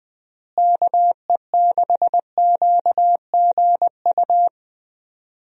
Morse code, heard as a keyed tone, about 20 wpm, 700 Hz. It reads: KE6QGU
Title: USA Call Signs - 6 Characters - Repeated 3x - 20wpm